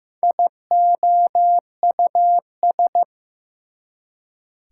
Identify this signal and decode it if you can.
Morse code, heard as a keyed tone, about 15 wpm, 700 Hz. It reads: IOUS
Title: Prefixes and Suffixes 15wpm